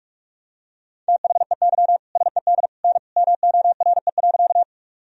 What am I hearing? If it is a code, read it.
Morse code, heard as a keyed tone, about 46 wpm, 700 Hz. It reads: THEY SED N MORE.
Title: Sentences from Top 100 Words and Abbreviations 45wpm